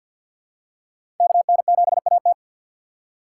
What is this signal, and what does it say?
Morse code, heard as a keyed tone, about 52 wpm, 700 Hz. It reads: KN7AT
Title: USA Call Signs - 5 Characters - 6x Character Spacing - 50wpm